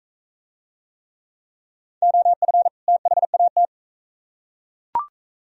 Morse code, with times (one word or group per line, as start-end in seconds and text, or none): 2.0-2.7 OF
2.9-3.6 THAT